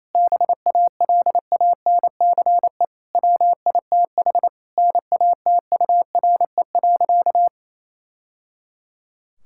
Morse code, heard as a keyed tone, about 28 wpm, 700 Hz. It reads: BALANCE WITH NATURE.
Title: Sentences from Top 1000 Words - Review Part 6 - 28wpm